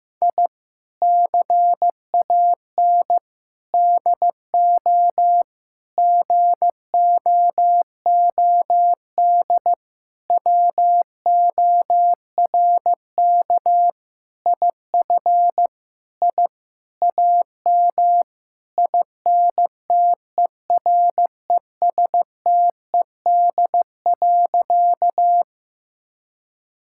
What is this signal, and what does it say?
Morse code, heard as a keyed tone, about 15 wpm, 700 Hz. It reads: I CAN DO GOOD WORK IF I AM INTERESTED.
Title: Sentences from Top 1000 Words - Review Part 6 - 15wpm